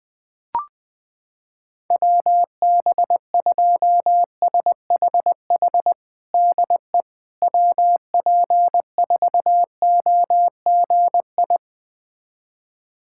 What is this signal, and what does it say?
Morse code, heard as a keyed tone, about 20 wpm, 700 Hz. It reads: WB2SHH DE WP4OGI
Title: USA Callsign DE Callsign - 6 Characters - 20wpm